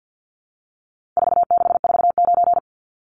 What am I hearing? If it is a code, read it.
Morse code, heard as a keyed tone, about 50 wpm, 700 Hz. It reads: 4649
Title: Numbers - 4-Digit 50wpm